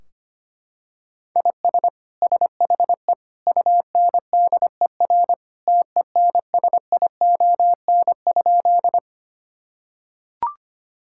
Morse code, heard as a keyed tone, about 25 wpm, 700 Hz. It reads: IS SHE UNDER TENSION?
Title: Sentences from Top 1600 Words 25wpm